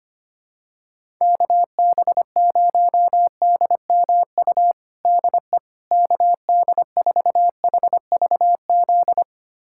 Morse code, heard as a keyed tone, about 25 wpm, 700 Hz. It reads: KB0DMU DE KD4HVZ
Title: USA Callsign DE Callsign 25wpm